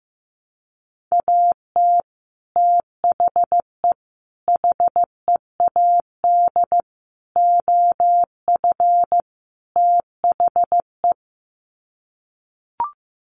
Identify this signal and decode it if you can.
Morse code, heard as a keyed tone, about 15 wpm, 700 Hz. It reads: AT THE HEAD OF THE
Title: Sets of 5 Words 15wpm